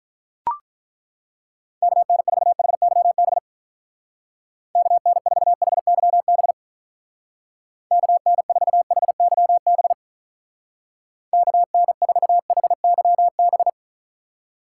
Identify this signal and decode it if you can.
Morse code, heard as a keyed tone, about 42 wpm, 700 Hz. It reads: KN4HYB KN4HYB KN4HYB KN4HYB
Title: Speed Racing - USA 2x3 Call Signs 35wpm